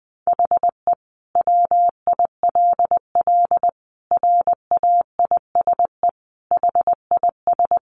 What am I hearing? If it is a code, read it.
Morse code, heard as a keyed tone, about 20 wpm, 700 Hz. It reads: HE WILL RAISE HIS
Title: Sentences from Top 1400 Words 20wpm